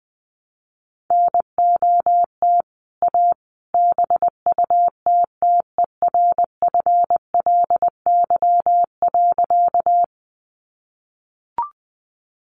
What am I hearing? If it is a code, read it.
Morse code, heard as a keyed tone, about 21 wpm, 700 Hz. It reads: NOT A BUTTERFLY.
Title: A Sound Of Thunder 20wpm